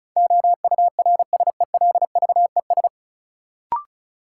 Morse code, heard as a keyed tone, about 36 wpm, 700 Hz. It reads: OURSELVES
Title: Sentences from Top 1600 Words - New Word Drill - 35wpm